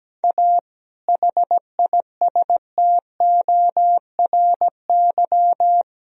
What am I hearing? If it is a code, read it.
Morse code, heard as a keyed tone, about 17 wpm, 700 Hz. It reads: A HISTORY
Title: Sentences from Top 1000 Words - Review Part 3 - 17wpm